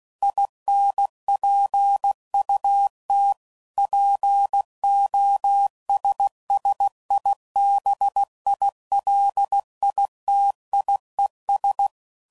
Morse code, heard as a keyed tone, about 16 wpm, 800 Hz. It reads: INPUT POSSIBILITIES
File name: CW Code Groups @15WPM / Technical Subjects / OR Gate